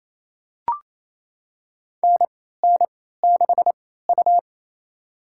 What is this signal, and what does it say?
Morse code, heard as a keyed tone, about 28 wpm, spaced as for 16 wpm, 700 Hz. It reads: NN6U
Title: USA Call Signs - 4 Characters - 3x Character Spacing - 28wpm